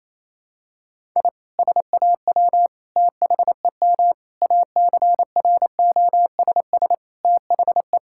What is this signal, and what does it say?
Morse code, heard as a keyed tone, about 28 wpm, 700 Hz. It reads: I SAW THEM ACROSS THE
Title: Sentences from Top 1000 Words - Review Part 5 - 28wpm